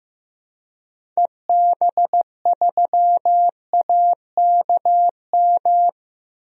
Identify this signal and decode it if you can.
Morse code, heard as a keyed tone, about 15 wpm, 700 Hz. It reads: EB3AKM
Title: DXCC-100 Call Signs - 6 Characters - Repeated 3x - 15wpm